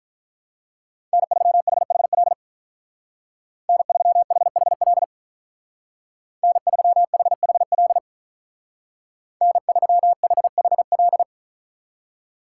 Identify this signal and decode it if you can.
Morse code, heard as a keyed tone, about 43 wpm, 700 Hz. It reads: N3HHL N3HHL N3HHL N3HHL
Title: Speed Racing - USA 1x3 Call Signs 35wpm